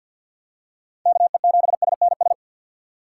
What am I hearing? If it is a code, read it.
Morse code, heard as a keyed tone, about 52 wpm, 700 Hz. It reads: KE7SNS
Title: USA Call Signs - 6 Characters - 6x Character Spacing - 50wpm